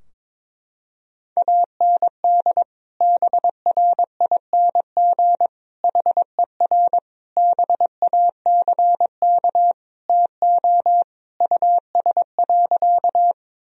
Morse code, heard as a keyed tone, about 22 wpm, 700 Hz. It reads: AND BRING HER BACK TO US.
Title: Sentences from Top 1000 Words - Review Part 3 - 22wpm